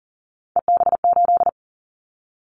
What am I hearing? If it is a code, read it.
Morse code, heard as a keyed tone, about 38 wpm, 700 Hz. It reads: EB8
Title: Callsign Prefixes for Top 50 DXCC - 40wpm